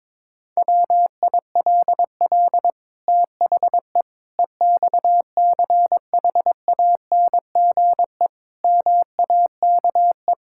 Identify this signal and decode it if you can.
Morse code, heard as a keyed tone, about 22 wpm, 700 Hz. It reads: WILL THE EXCHANGE MAKE
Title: Sentences from Top 1000 Words - Review Part 6 - 22wpm